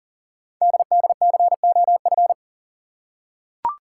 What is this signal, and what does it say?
Morse code, heard as a keyed tone, about 37 wpm, 700 Hz. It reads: DDCOF
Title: Code Groups - 5 Letters - 40wpm